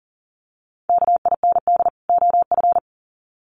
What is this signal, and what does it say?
Morse code, heard as a keyed tone, about 40 wpm, 700 Hz. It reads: KIND OF
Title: Sets of 2 Words 40wpm